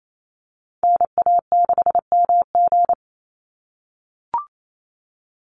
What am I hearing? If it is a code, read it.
Morse code, heard as a keyed tone, about 28 wpm, 700 Hz. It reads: NA6MG
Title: NA SOTA Activator Callsigns 28wpm